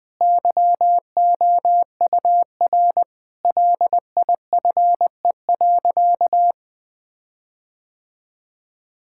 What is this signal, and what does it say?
Morse code, heard as a keyed tone, about 19 wpm, 700 Hz. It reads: YOUR LIFE.
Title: Sentences with 5-Word Phrases - Spoken Beginning - 20wpm